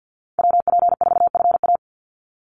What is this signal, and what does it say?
Morse code, heard as a keyed tone, about 48 wpm, 700 Hz. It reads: WP4FA
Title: USA Call Signs 50wpm